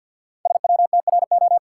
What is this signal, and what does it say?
Morse code, heard as a keyed tone, about 52 wpm, 700 Hz. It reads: SWTRO
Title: Code Groups - 5 Letters - 50wpm